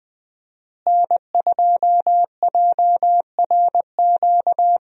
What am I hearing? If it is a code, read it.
Morse code, heard as a keyed tone, about 20 wpm, 700 Hz. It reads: N2JRQ
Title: USA Call Signs - 5 Characters - Repeated 3x - 20wpm